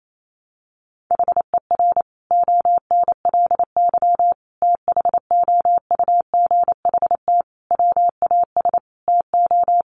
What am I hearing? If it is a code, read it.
Morse code, heard as a keyed tone, about 28 wpm, 700 Hz. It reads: HER ONLY THOUGHT WAS TO
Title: Sentences from Top 400 Words 28wpm